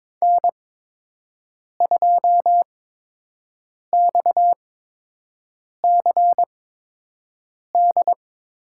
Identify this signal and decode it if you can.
Morse code, heard as a keyed tone, about 23 wpm, spaced as for 6 wpm, 700 Hz. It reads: N2XCD
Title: USA Call Signs - 5 Characters - 8x Character Spacing - 22wpm